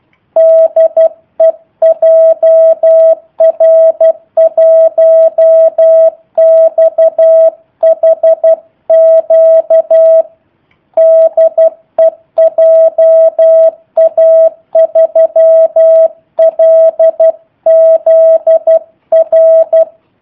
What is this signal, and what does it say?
Morse code, heard as a keyed tone, about 12 wpm, 650 Hz. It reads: DE JR1XHQ DE JA3LZR
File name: jh8cbh / 2019cw / sp03- w